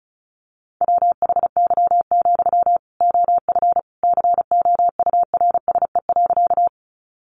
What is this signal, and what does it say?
Morse code, heard as a keyed tone, about 34 wpm, 700 Hz. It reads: WHY, OF COURSE.
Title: Sentences from Top 500 Words 35wpm